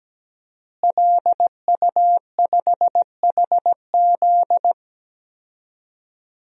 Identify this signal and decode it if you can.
Morse code, heard as a keyed tone, about 17 wpm, 700 Hz. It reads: LU5HZ
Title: DXCC-100 Call Signs - 5 Characters - Repeated 2x - 17wpm